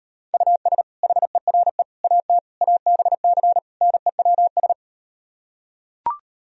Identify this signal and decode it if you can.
Morse code, heard as a keyed tone, about 38 wpm, 700 Hz. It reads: US HERE AT ABC NEWS